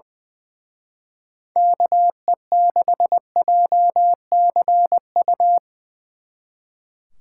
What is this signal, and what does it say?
Morse code, heard as a keyed tone, about 22 wpm, 700 Hz. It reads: KE6JCU
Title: Speed Racing - USA 2x3 Call Signs 20wpm